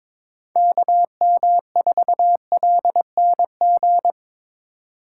Morse code, heard as a keyed tone, about 21 wpm, 700 Hz. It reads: KM4LNG
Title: USA Call Signs - 6 Characters - 4x Character Spacing - 22wpm